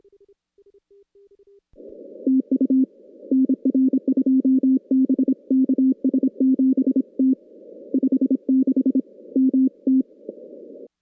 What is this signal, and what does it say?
Morse code, heard as a keyed tone, about 26 wpm, 260 Hz. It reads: TU NR 2 B KS7T 56 MT
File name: contests / CW-SS-k5tr- -2100